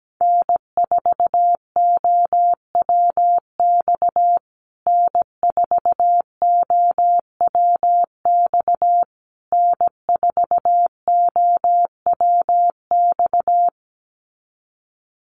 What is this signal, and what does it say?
Morse code, heard as a keyed tone, about 17 wpm, 700 Hz. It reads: N4OWX N4OWX N4OWX
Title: USA Call Signs Repeated 3x 17wpm